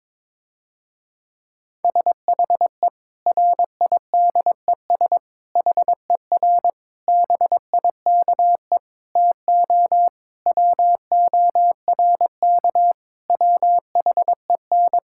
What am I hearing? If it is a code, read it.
Morse code, heard as a keyed tone, about 22 wpm, 700 Hz. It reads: SHE RIDES HER BIKE TO WORK WHEN
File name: Difficult Sentences with Hints - 22wpm